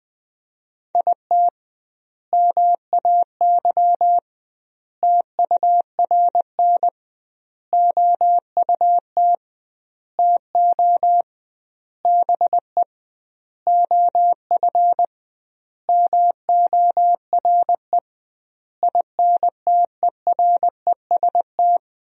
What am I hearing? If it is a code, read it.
Morse code, heard as a keyed tone, about 20 wpm, 700 Hz. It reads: IT MAY TURN OUT TO BE OF MORE INTEREST
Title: 2x Word-Spacing - Sentences from Top 400 Words - 20wpm